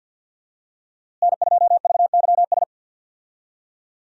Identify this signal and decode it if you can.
Morse code, heard as a keyed tone, about 52 wpm, 700 Hz. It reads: N2VYS
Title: USA Call Signs - 5 Characters - Repeated 2x - 50wpm